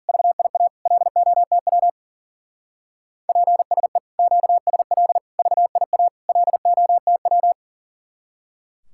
Morse code, heard as a keyed tone, about 40 wpm, 700 Hz. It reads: VIA LOTW PSE QSL VIA LOTW
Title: Speed-Racing Common QSO Phrases - 40wpm